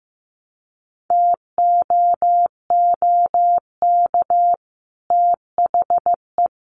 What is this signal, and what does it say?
Morse code, heard as a keyed tone, about 15 wpm, 700 Hz. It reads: TOOK THE
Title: Sentences from Top 1300 Words 15wpm